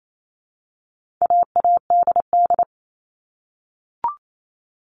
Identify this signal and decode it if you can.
Morse code, heard as a keyed tone, about 28 wpm, 700 Hz. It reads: AADD
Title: All Cut Numbers - 4-Digit 28wpm